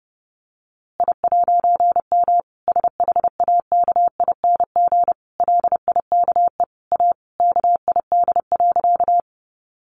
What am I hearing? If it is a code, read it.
Morse code, heard as a keyed tone, about 30 wpm, 700 Hz. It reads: I'M SHAKING LIKE A KID.
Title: A Sound Of Thunder 30wpm